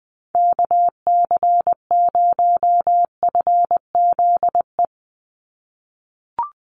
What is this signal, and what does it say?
Morse code, heard as a keyed tone, about 20 wpm, 700 Hz. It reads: KC0FZE
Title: USA Call Signs 20wpm